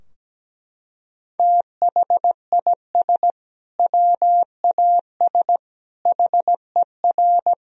Morse code, heard as a keyed tone, about 17 wpm, 700 Hz. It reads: THIS WAS HER
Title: Sentences from Top 1000 Words - Review Part 3 - 17wpm